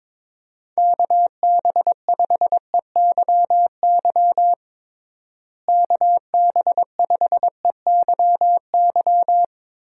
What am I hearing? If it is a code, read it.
Morse code, heard as a keyed tone, about 23 wpm, 700 Hz. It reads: KB5EYY KB5EYY
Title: USA Call Signs - 6 Characters - Repeated 2x - 22wpm